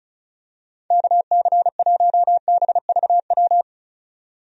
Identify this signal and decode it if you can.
Morse code, heard as a keyed tone, about 34 wpm, 700 Hz. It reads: KC1BVW
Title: USA Call Signs - 6 Characters - 35wpm